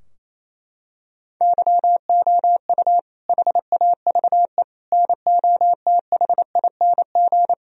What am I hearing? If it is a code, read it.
Morse code, heard as a keyed tone, about 27 wpm, 700 Hz. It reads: YOU HAVE NOTHING
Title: Sentences from Top 1000 Words - Review Part 3 - 28wpm